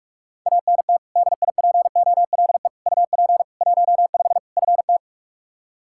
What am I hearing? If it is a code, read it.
Morse code, heard as a keyed tone, about 46 wpm, 700 Hz. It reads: ANT DIPOLE UP 15 FT
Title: Practice QSO - Exchange 2 - Info - 45wpm